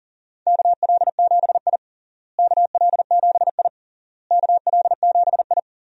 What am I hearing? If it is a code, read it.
Morse code, heard as a keyed tone, about 40 wpm, 700 Hz. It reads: KL7I KL7I KL7I
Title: DXCC-100 Call Signs - 4 Characters - Repeated 3x - 40wpm